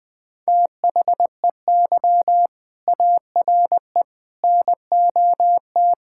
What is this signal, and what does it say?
Morse code, heard as a keyed tone, about 20 wpm, 700 Hz. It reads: THEY ARE NOT
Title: Sentences from Top 1000 Words - Review Part 5 - 20wpm